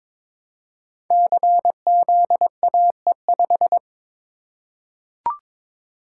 Morse code, heard as a keyed tone, about 22 wpm, 700 Hz. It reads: CZAE5
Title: Code Groups - 5 Letter:Numbers - 22wpm